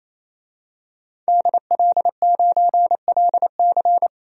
Morse code, heard as a keyed tone, about 28 wpm, 700 Hz. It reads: DL9LC
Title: DXCC-100 Call Signs - 5 Characters - 28wpm